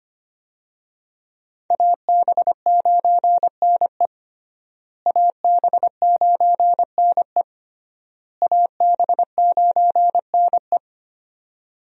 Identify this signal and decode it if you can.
Morse code, heard as a keyed tone, about 25 wpm, 700 Hz. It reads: AB9NE AB9NE AB9NE
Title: USA Call Signs - 5 Characters - Repeated 3x - 25wpm